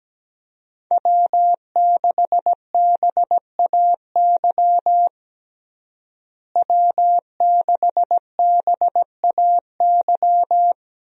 Morse code, heard as a keyed tone, about 17 wpm, 700 Hz. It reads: W6BAY W6BAY
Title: USA Call Signs - 5 Characters - Repeated 2x - 17wpm